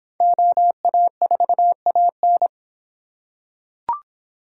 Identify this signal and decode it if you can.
Morse code, heard as a keyed tone, about 25 wpm, 700 Hz. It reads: OA4AN